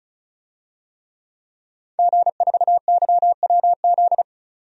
Morse code, heard as a keyed tone, about 36 wpm, 700 Hz. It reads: G4YWZ
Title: DXCC-100 Call Signs - 5 Characters - 35wpm